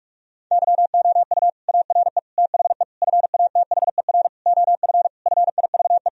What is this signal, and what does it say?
Morse code, heard as a keyed tone, about 41 wpm, 700 Hz. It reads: YOU ARE THE FATHER OF FIVE
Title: Sentences from Top 1000 Words - Review Part 2 - 45wpm